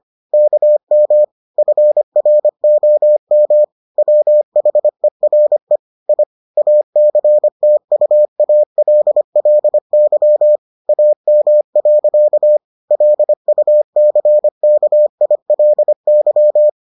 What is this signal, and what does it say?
Morse code, heard as a keyed tone, about 25 wpm, 600 Hz. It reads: KM FROM WHERE I ACTUALLY AM. LUCKILY